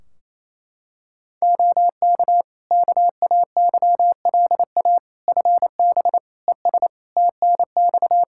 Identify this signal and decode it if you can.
Morse code, heard as a keyed tone, about 28 wpm, 700 Hz. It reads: OK KAYLA FB ES TNX
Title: K1USN SST QSO Practice 28wpm